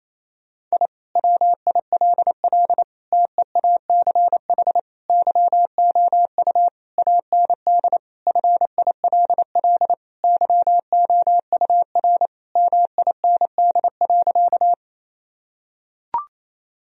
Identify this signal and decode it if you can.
Morse code, heard as a keyed tone, about 27 wpm, 700 Hz. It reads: I WILL TEACH YOU AND FILL YOUR MIND.
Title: Sentences from Top 1000 Words - Review Part 3 - 28wpm